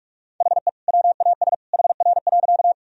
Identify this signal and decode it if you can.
Morse code, heard as a keyed tone, about 46 wpm, 700 Hz. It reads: HE WAS HR.
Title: Sentences from Top 200 Words and Abbreviations 45wpm